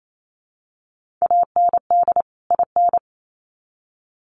Morse code, heard as a keyed tone, about 28 wpm, 700 Hz. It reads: AND IN
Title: Sets of 2 Words 28wpm